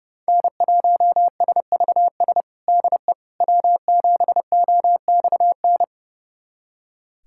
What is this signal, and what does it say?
Morse code, heard as a keyed tone, about 30 wpm, 700 Hz. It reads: N1SVS DE W7OXN
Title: USA Callsign DE Callsign - 5 Characters - 30wpm